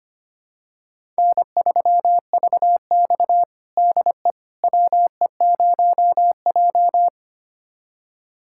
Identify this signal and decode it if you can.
Morse code, heard as a keyed tone, about 25 wpm, 700 Hz. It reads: N3VX DE WE0J
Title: USA Callsign DE Callsign - 4 Characters - 25wpm